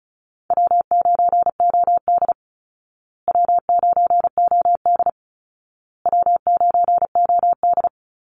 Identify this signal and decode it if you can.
Morse code, heard as a keyed tone, about 34 wpm, 700 Hz. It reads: W9OD W9OD W9OD
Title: Call Signs Repeated 2x - 35wpm